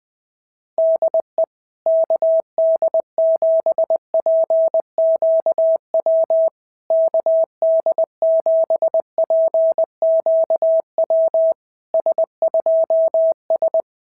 Morse code, heard as a keyed tone, about 20 wpm, 660 Hz. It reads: DE KD7PQW KD7PQW S2S
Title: SOTA QSO Practice 20wpm